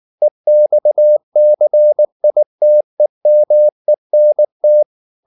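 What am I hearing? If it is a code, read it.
Morse code, heard as a keyed tone, about 19 wpm, 600 Hz. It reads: EXCITEMENT
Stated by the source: Morse code at 19 wpm from CW med Gnister, reading Ken Follett's novel Never in English.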